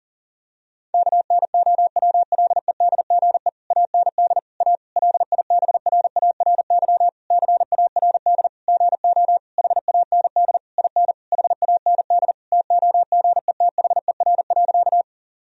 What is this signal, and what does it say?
Morse code, heard as a keyed tone, about 40 wpm, 700 Hz. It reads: KNOWLEDGE AND A LIBRARY CARD GO HAND IN HAND TOGETHER.
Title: Sentences from Top 1000 Words - Review Part 4 - 40wpm